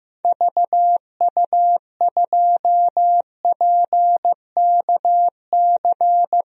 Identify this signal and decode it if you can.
Morse code, heard as a keyed tone, about 16 wpm, 700 Hz. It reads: VU2PKC
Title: DXCC-100 Call Signs - 6 Characters - Repeated 3x - 15wpm